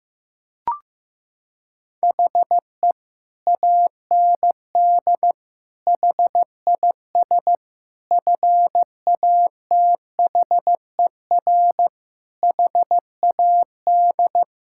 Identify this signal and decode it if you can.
Morse code, heard as a keyed tone, about 15 wpm, 700 Hz. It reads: HE AND HIS FATHER HAD
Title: Sentences from Top 1600 Words 15wpm